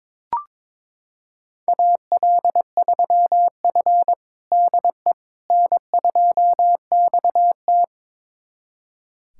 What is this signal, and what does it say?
Morse code, heard as a keyed tone, about 21 wpm, 700 Hz. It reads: AL3F DE N2XT
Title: USA Callsign DE Callsign - 4 Characters - 22wpm